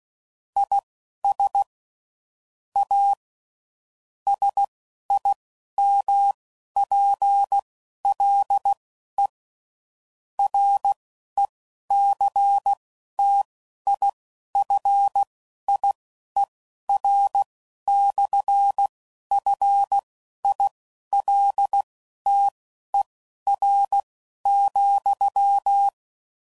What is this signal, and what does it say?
Morse code, heard as a keyed tone, about 16 wpm, spaced as for 11 wpm, 800 Hz. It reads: IS A SIMPLE RECTIFIER/FILTER,
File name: CW Code Groups @10WPM / Technical Subjects / SwPwrSup